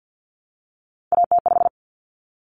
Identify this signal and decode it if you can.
Morse code, heard as a keyed tone, about 50 wpm, 700 Hz. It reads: AT5
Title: Common Cut Numbers - 3-Digit 50wpm